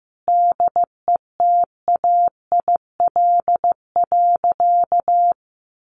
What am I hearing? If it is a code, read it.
Morse code, heard as a keyed tone, about 15 wpm, 700 Hz. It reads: DETAIL.
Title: Sentences from Top 1500 Words 15wpm